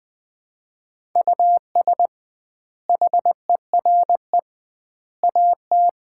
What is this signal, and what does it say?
Morse code, heard as a keyed tone, about 20 wpm, 700 Hz. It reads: US HERE AT
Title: 2x Word-Spacing - Sets of 5 Words - 20wpm